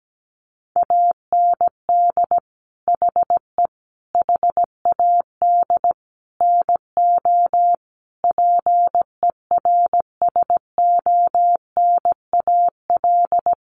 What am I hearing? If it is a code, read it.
Morse code, heard as a keyed tone, about 17 wpm, 700 Hz. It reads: AND HE HAD NO PERSONAL
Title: Sentences from Top 1400 Words 17wpm